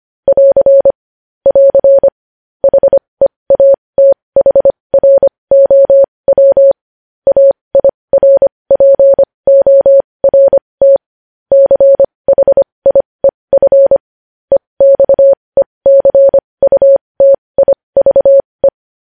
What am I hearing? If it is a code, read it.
Morse code, heard as a keyed tone, about 25 wpm, spaced as for 20 wpm, 550 Hz. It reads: <AR> <AR> HEATHROW AIRPORT CHIEF EXECUTIVE
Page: News Headlines in Morse Code at 20 WPM